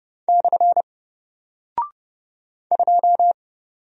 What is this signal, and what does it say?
Morse code, heard as a keyed tone, about 31 wpm, 700 Hz. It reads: / 2
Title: Single Character (Letter:Number:Punctuation:Prosign) 30wpm